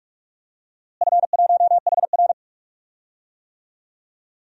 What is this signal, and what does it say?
Morse code, heard as a keyed tone, about 46 wpm, 700 Hz. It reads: F1HR
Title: DXCC-100 Call Signs - 4 Characters - Repeated 3x - 45wpm